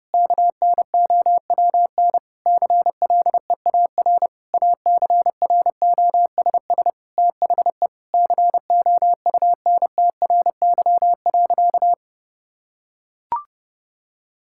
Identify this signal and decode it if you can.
Morse code, heard as a keyed tone, about 30 wpm, 700 Hz. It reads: KNOWN CLEAR ACROSS THE COUNTRY.
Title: Sentences from Top 1000 Words - Review Part 5 - 30wpm